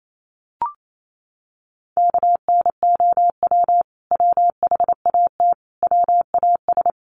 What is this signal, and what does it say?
Morse code, heard as a keyed tone, about 28 wpm, 700 Hz. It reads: KNOW WHAT WAS
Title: Sets of 5 Words Encore2 28wpm